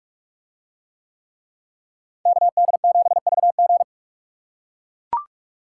Morse code, heard as a keyed tone, about 44 wpm, 700 Hz. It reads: KD7VG